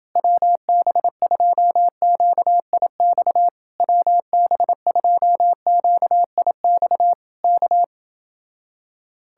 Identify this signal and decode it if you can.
Morse code, heard as a keyed tone, about 26 wpm, 700 Hz. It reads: WB2QIX WB2QIX K